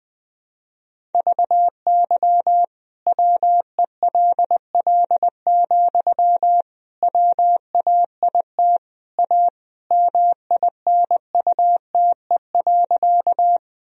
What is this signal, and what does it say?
Morse code, heard as a keyed tone, about 20 wpm, 700 Hz. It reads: VY WELL, WAIT A MINUTE.
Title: Sentences from Top 500 Words and Abbreviations 20wpm